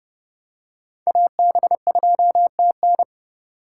Morse code, heard as a keyed tone, about 32 wpm, 700 Hz. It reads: AB2TN
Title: USA Call Signs - 5 Characters - 3x Character Spacing - 30wpm